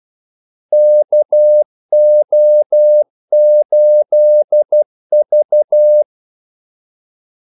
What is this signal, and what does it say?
Morse code, heard as a keyed tone, about 12 wpm, 600 Hz. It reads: KO8V
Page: 2 ekstra ordmellemrum mellem kaldesignalerne